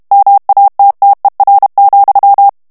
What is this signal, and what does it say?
Morse code, heard as a keyed tone, about 32 wpm, 800 Hz. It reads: MATTER,
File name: morse mp3 / ANGLAIS texte clair 3 vitesse 30 mots minute